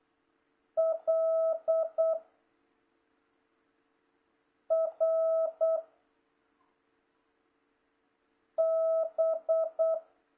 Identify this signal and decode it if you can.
Morse code, heard as a keyed tone, about 8 wpm, 660 Hz. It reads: L R B